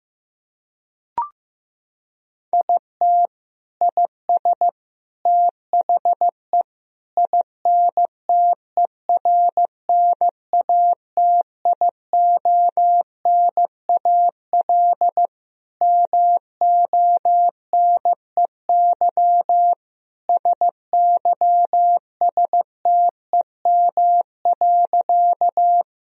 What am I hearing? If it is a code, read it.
Morse code, heard as a keyed tone, about 15 wpm, 700 Hz. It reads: IT IS THE INTERNATIONAL MONEY SYSTEM.
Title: Sentences from Top 1000 Words - Review Part 4 - 15wpm